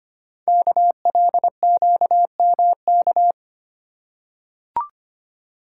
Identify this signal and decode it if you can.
Morse code, heard as a keyed tone, about 25 wpm, 700 Hz. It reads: KLQMK
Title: Code Groups - 5 Letters - 25wpm